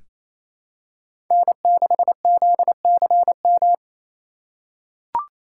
Morse code, heard as a keyed tone, about 28 wpm, 700 Hz. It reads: N6ZCM
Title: USA Call Signs - 5 Characters - 2x Character Spacing - 28wpm